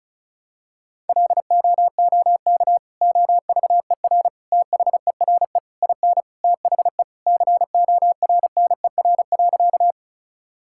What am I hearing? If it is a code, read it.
Morse code, heard as a keyed tone, about 34 wpm, 700 Hz. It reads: LOOK OVER THERE IN THE CORNER.
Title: Sentences from Top 1600 Words 35wpm